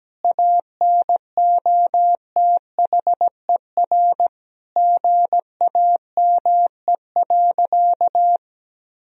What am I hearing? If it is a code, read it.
Morse code, heard as a keyed tone, about 17 wpm, 700 Hz. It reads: ANOTHER GAME.
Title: Sentences from Top 1000 Words - Review Part 4 - 17wpm